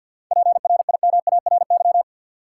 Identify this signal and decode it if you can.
Morse code, heard as a keyed tone, about 50 wpm, 700 Hz. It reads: PRIMARY
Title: Seven Letter Words - 50wpm